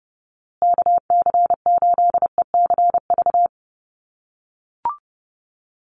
Morse code, heard as a keyed tone, about 30 wpm, 700 Hz. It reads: KC8ECV
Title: USA Call Signs Repeated 2x 30wpm